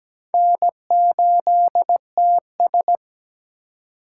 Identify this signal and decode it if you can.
Morse code, heard as a keyed tone, about 21 wpm, 700 Hz. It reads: N8TS
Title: Speed Racing - USA 1x2 Call Signs 17wpm